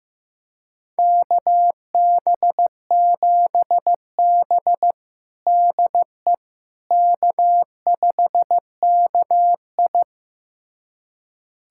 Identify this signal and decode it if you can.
Morse code, heard as a keyed tone, about 15 wpm, 700 Hz. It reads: KB7B DE K5KI
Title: USA Callsign DE Callsign - 4 Characters - 15wpm